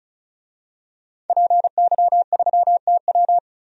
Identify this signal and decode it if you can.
Morse code, heard as a keyed tone, about 36 wpm, 700 Hz. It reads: PY3TW
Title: DXCC-100 Call Signs - 5 Characters - 35wpm